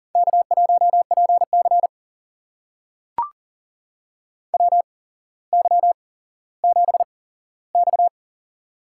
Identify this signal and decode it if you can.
Morse code, heard as a keyed tone, about 41 wpm, spaced as for 17 wpm, 700 Hz. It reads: K1PC WY7X
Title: USA Call Signs - 4 Characters - 8x Character Spacing - 40wpm